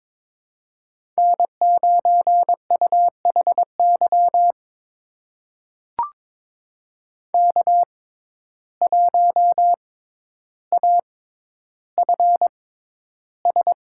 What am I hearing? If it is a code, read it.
Morse code, heard as a keyed tone, about 22 wpm, spaced as for 11 wpm, 700 Hz. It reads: N9UHY K1AFS
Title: USA Call Signs - 5 Characters - 6x Character Spacing - 22wpm